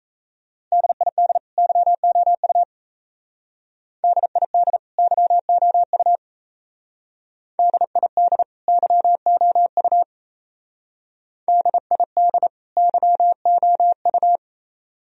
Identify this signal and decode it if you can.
Morse code, heard as a keyed tone, about 36 wpm, 700 Hz. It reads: DID YOU DID YOU DID YOU DID YOU